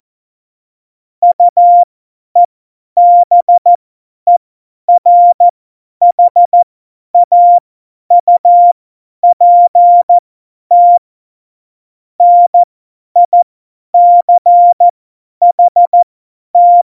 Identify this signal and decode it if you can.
Morse code, heard as a keyed tone, about 14 wpm, spaced as for 10 wpm, 700 Hz. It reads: UEBERHAUPT NICHT
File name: Morsekurs / Witze / Witz 27. August 2019, 50 F70